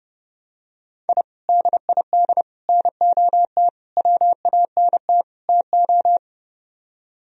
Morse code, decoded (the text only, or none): I DID NOT WANT TO